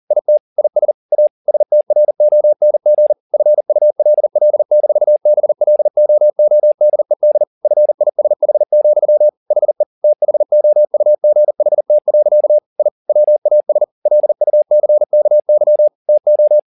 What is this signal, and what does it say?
Morse code, heard as a keyed tone, about 40 wpm, 600 Hz. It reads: IT IS A STRONG FULL-BLOODED FISH, HE THOUGHT. I WAS LUCKY TO